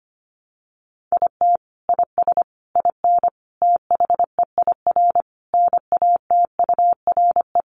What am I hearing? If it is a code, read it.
Morse code, heard as a keyed tone, about 25 wpm, 700 Hz. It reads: IT IS IN THEIR NATURE
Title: Sentences from Top 1300 Words 25wpm